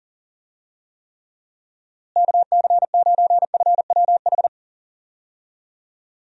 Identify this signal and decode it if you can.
Morse code, heard as a keyed tone, about 38 wpm, 700 Hz. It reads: KC9FWH
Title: USA Call Signs - 6 Characters - Repeated 2x - 40wpm